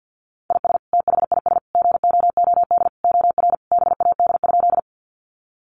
Morse code, heard as a keyed tone, about 50 wpm, 700 Hz. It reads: IS THIS GOOD OR BAD?
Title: Sentences from Top 700 Words 50wpm